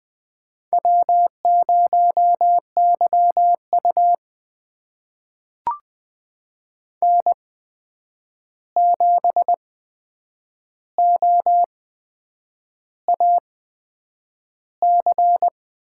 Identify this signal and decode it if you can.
Morse code, heard as a keyed tone, about 20 wpm, spaced as for 8 wpm, 700 Hz. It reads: W0YU N7OAC
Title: USA Call Signs - 8x Character Spacing - 20wpm